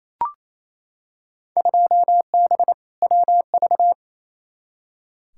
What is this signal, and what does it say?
Morse code, heard as a keyed tone, about 28 wpm, 700 Hz. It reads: 2B WV
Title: Field Day Exchanges Encore - 28wpm